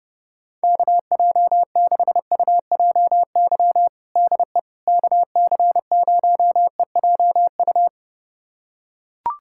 Morse code, heard as a keyed tone, about 30 wpm, 700 Hz. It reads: KJ6UJY DE KC0EJU
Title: USA Callsign DE Callsign - 6 Characters - 30wpm